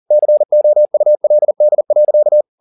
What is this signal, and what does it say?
Morse code, heard as a keyed tone, about 40 wpm, 600 Hz. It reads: COULD.
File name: the old man and the sea 2 40wpm